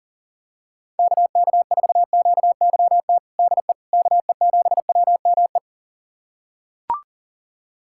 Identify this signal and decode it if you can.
Morse code, heard as a keyed tone, about 40 wpm, 700 Hz. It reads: KK4QYT DE KE7WME
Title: USA Callsign DE Callsign - 6 Characters - 40wpm